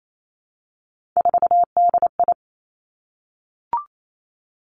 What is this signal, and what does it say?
Morse code, heard as a keyed tone, about 28 wpm, 700 Hz. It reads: 4DI
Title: Three Challenging Letter-Number Combinations 28wpm